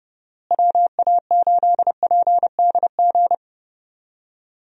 Morse code, heard as a keyed tone, about 31 wpm, 700 Hz. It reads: WA8PDG
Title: USA Call Signs - 6 Characters - Repeated 3x - 30wpm